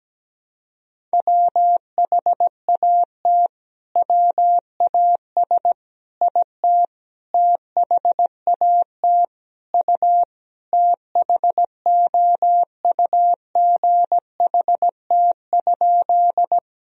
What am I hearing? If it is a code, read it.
Morse code, heard as a keyed tone, about 17 wpm, 700 Hz. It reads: WHAT WAS IT THAT U THOUGHT?
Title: Sentences from Top 300 Words and Abbreviations 17wpm